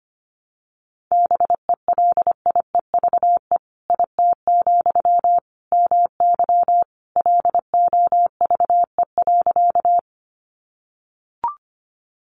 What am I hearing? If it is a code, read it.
Morse code, heard as a keyed tone, about 25 wpm, 700 Hz. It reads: BELIEVE IT, MY LOVE.
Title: Sentences from Top 1000 Words 25wpm